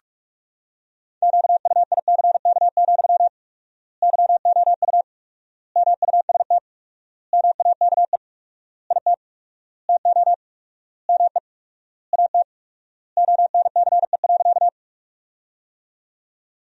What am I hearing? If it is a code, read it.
Morse code, heard as a keyed tone, about 43 wpm, 700 Hz. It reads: QUICK, YOU MUST MAKE IT TO ME AT ONCE.
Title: 4x Word-Spacing - Sentences from Top 500 Words - 45wpm